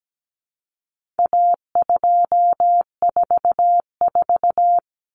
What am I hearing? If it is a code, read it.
Morse code, heard as a keyed tone, about 17 wpm, 700 Hz. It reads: A244
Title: Common Cut Numbers - 4-Digit 17wpm